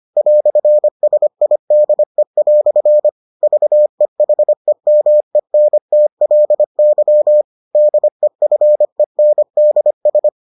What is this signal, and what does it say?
Morse code, heard as a keyed tone, about 25 wpm, 600 Hz. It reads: "SIDE" VEHEMENTLY DEFENDS